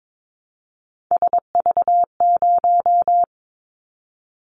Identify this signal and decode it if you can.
Morse code, heard as a keyed tone, about 22 wpm, 700 Hz. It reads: SV0
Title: Callsign Prefixes for Top 50 DXCC - 22wpm